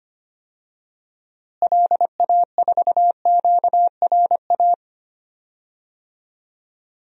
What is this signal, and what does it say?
Morse code, heard as a keyed tone, about 25 wpm, 700 Hz. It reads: LA4QRA
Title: DXCC-100 Call Signs - 6 Characters - Repeated 3x - 25wpm